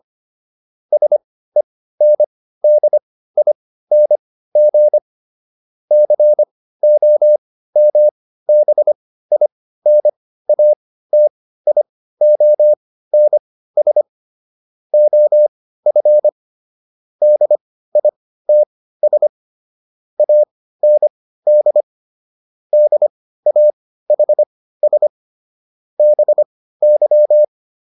Morse code, heard as a keyed tone, about 25 wpm, spaced as for 15 wpm, 600 Hz. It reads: SENDING COMBINATIONS OF DITS AND DAHS BY
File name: SS215 15